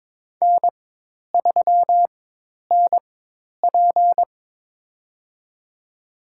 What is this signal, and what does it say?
Morse code, heard as a keyed tone, about 22 wpm, spaced as for 10 wpm, 700 Hz. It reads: N3NP
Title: USA Call Signs - 4 Characters - 4x Character Spacing - 22wpm